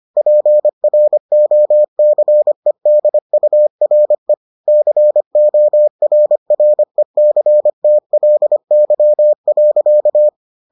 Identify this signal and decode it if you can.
Morse code, heard as a keyed tone, about 25 wpm, 600 Hz. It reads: PROCEDURE CORRECTLY.